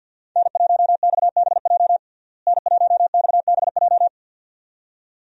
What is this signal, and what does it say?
Morse code, heard as a keyed tone, about 50 wpm, 700 Hz. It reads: N1XBJ N1XBJ
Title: USA Call Signs - 5 Characters - Repeated 2x - 50wpm